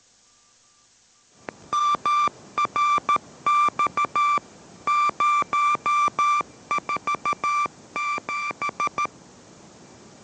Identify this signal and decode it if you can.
Morse code, heard as a keyed tone, about 13 wpm, 1180 Hz. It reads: MRX 047